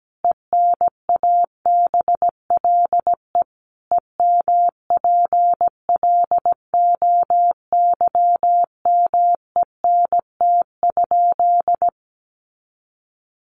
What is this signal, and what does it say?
Morse code, heard as a keyed tone, about 17 wpm, 700 Hz. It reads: ENABLE EMPLOYMENT?
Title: Sentences from Top 1500 Words 17wpm